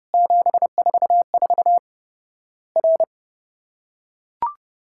Morse code, heard as a keyed tone, about 32 wpm, 700 Hz. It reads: 744 R
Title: Worked All Europe DX Contest - QTC Exchange - 30wpm